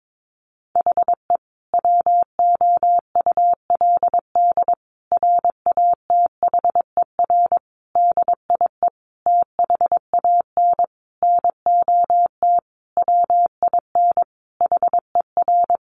Morse code, heard as a keyed tone, about 22 wpm, 700 Hz. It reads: HE WOULD RATHER DIE THAN NOT WIN HER
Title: Sentences from Top 1500 Words 22wpm